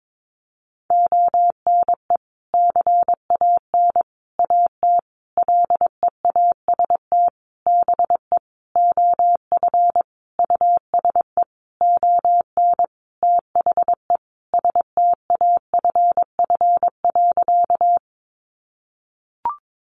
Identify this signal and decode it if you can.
Morse code, heard as a keyed tone, about 22 wpm, 700 Hz. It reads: ONE CAN AT LEAST BE OF USE ON THE STAFF.
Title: Sentences from Top 1200 Words 22wpm